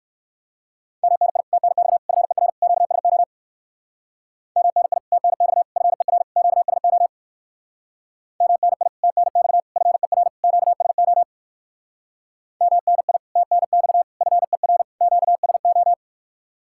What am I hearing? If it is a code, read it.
Morse code, heard as a keyed tone, about 59 wpm, 700 Hz. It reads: MNI TNX FER QSO MNI TNX FER QSO MNI TNX FER QSO MNI TNX FER QSO